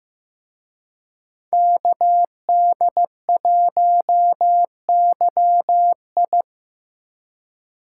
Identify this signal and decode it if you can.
Morse code, heard as a keyed tone, about 18 wpm, 700 Hz. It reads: KD1YI
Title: Speed Racing - USA Call Signs 15wpm